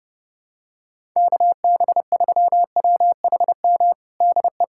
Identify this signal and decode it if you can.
Morse code, heard as a keyed tone, about 32 wpm, 700 Hz. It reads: KB3WHM DE
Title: USA Callsign DE Callsign - 6 Characters - 30wpm